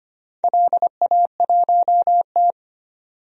Morse code, heard as a keyed tone, about 25 wpm, 700 Hz. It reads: LA1T
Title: DXCC-100 Call Signs - 4 Characters - 25wpm